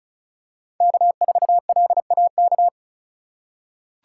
Morse code, heard as a keyed tone, about 36 wpm, 700 Hz. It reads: K4LAK
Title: USA Call Signs - 5 Characters - 4x Character Spacing - 35wpm